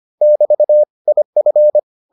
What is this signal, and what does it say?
Morse code, heard as a keyed tone, about 25 wpm, 600 Hz. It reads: <BT> IF